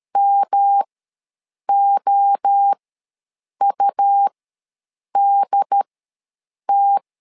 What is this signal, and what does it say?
Morse code, heard as a keyed tone, about 13 wpm, spaced as for 7 wpm, 800 Hz. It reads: MOUDT
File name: LEZIONI / 3 MWRO / 3 MWRO ws 12wpm 1914